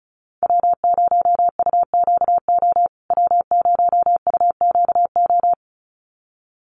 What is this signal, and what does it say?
Morse code, heard as a keyed tone, about 34 wpm, 700 Hz. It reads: W0UQO W0UQO
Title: USA Call Signs Repeated 2x 35wpm